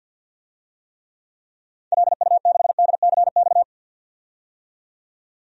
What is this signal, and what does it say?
Morse code, heard as a keyed tone, about 50 wpm, 700 Hz. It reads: LU6DCX
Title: DXCC-100 Call Signs - 6 Characters - 50wpm